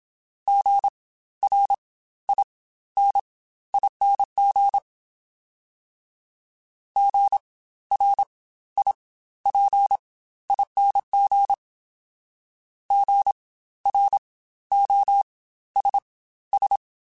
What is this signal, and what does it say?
Morse code, heard as a keyed tone, about 27 wpm, spaced as for 14 wpm, 800 Hz. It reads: GRINING GRIPING GROSS
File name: Spaced Alphas / Spaced Alphas mp3s / Gs sp @25